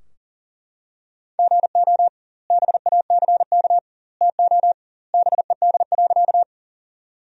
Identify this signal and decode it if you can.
Morse code, heard as a keyed tone, about 40 wpm, 700 Hz. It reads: GO BACK TO BED.
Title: 2x Word-Spacing - Sentences from Top 500 Words - 40wpm